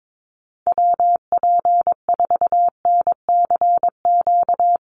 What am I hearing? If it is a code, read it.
Morse code, heard as a keyed tone, about 22 wpm, 700 Hz. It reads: WP4NCQ
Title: USA Call Signs Repeated 2x 22wpm